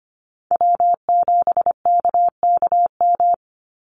Morse code, heard as a keyed tone, about 25 wpm, 700 Hz. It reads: W7KKM
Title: NA SOTA Chaser Callsigns 25wpm